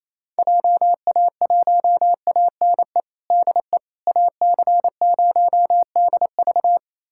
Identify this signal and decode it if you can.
Morse code, heard as a keyed tone, about 28 wpm, 700 Hz. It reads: JA1ANE DE AC0DV
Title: USA Callsign DE Callsign 28wpm